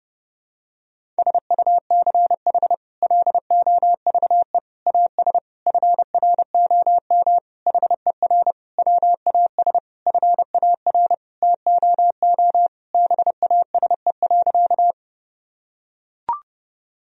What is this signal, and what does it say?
Morse code, heard as a keyed tone, about 32 wpm, 700 Hz. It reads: SUCH LOVE AS FROM HER WAS FAR TOO BASE.
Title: Sentences from Top 1000 Words - Review Part 1 - 30wpm